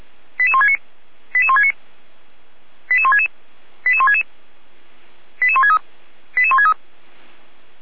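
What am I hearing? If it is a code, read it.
Five-tone selective call (ZVEI), heard as quick runs of five tones, tones of about 70 ms. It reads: 88169 88169 88166 88166 88163 88163
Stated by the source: ZVEI five-tone selective call 88367, received by SDR.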